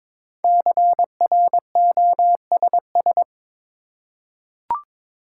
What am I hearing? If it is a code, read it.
Morse code, heard as a keyed tone, about 22 wpm, 700 Hz. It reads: CROSS
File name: 5 Letter Words - First 3 Letters Given - 22wpm